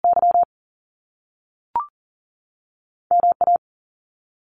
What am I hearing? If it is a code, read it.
Morse code, heard as a keyed tone, about 41 wpm, spaced as for 6 wpm, 700 Hz. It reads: YMA
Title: Top Two Letter Combinations in Words 40wpm